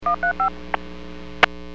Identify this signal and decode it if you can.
DTMF, heard as telephone touch tones, tones of about 110 ms. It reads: 135
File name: WPS 451.25